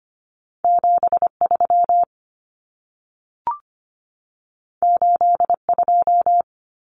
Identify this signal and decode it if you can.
Morse code, heard as a keyed tone, about 25 wpm, 700 Hz. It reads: 73 82
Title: Common Cut Numbers - 2-Digit 25wpm